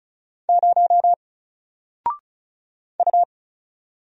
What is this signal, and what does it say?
Morse code, heard as a keyed tone, about 35 wpm, 700 Hz. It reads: 0 U
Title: Single Letter-Number Alt 35wpm